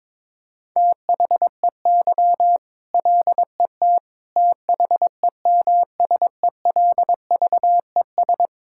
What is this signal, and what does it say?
Morse code, heard as a keyed tone, about 23 wpm, 700 Hz. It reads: THEY LET THEMSELVES
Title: Sentences from Top 1000 Words - Review Part 5 - 22wpm